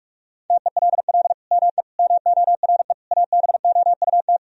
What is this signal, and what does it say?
Morse code, heard as a keyed tone, about 46 wpm, 700 Hz. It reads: TELL ME MORE ABOUT
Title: Sentences from Top 1000 Words - Review Part 3 - 45wpm